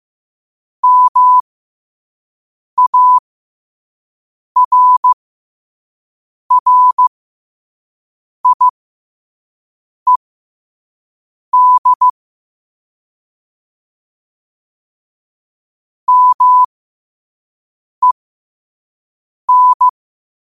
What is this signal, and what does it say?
Morse code, heard as a keyed tone, about 15 wpm, spaced as for 5 wpm, 1000 Hz. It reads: MARRIED MEN
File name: qotdfiles / 5wpm